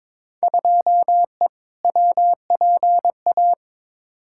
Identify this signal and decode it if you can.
Morse code, heard as a keyed tone, about 22 wpm, 700 Hz. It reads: 2E WPA
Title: Field Day Exchanges Encore - 22wpm